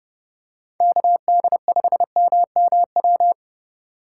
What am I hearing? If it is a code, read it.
Morse code, heard as a keyed tone, about 29 wpm, 700 Hz. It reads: KD5MMW